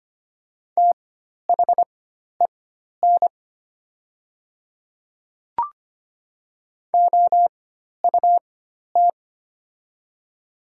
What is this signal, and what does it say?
Morse code, heard as a keyed tone, about 25 wpm, spaced as for 12 wpm, 700 Hz. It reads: THEN OUT
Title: 4x Character Spacing - Top 100 Words - 25wpm